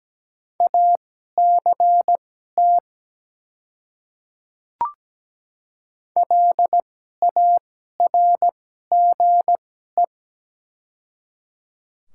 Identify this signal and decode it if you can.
Morse code, heard as a keyed tone, about 17 wpm, spaced as for 12 wpm, 700 Hz. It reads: ACT LARGE
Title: 2x Character Spacing - Top 200 Words - 17wpm